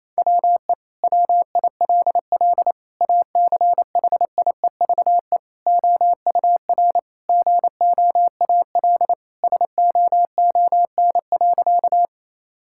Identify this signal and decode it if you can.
Morse code, heard as a keyed tone, about 28 wpm, 700 Hz. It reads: WE WILL ACHIEVE OUR GOAL SOON.
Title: Sentences from Top 1000 Words - Review Part 6 - 28wpm